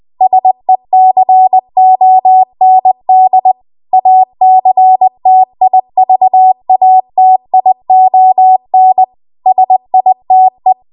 Morse code, heard as a keyed tone, about 20 wpm, 750 Hz. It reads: SECOND ACTIVATION SITE
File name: Archive / 20 WPM / 20WPM